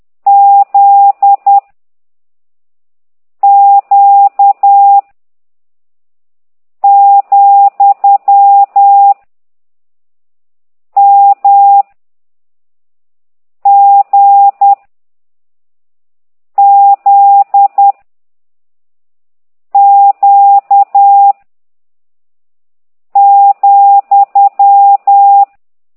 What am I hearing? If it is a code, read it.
Morse code, heard as a keyed tone, about 10 wpm, 800 Hz. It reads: Z Q , M G Z Q ,
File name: ct2idw / exerc15